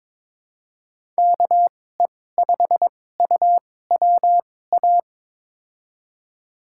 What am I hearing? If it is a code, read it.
Morse code, heard as a keyed tone, about 22 wpm, spaced as for 16 wpm, 700 Hz. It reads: KE5UWA
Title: USA Call Signs - 6 Characters - 2x Character Spacing - 22wpm